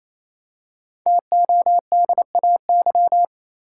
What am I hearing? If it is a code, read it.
Morse code, heard as a keyed tone, about 28 wpm, 700 Hz. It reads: TODAY
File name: Easy Words - Speed-Racing with Hints - 22wpm